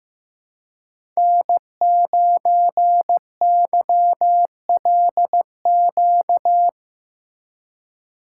Morse code, heard as a keyed tone, about 16 wpm, 700 Hz. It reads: N9YLQ
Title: USA Call Signs - 5 Characters - 6x Character Spacing - 15wpm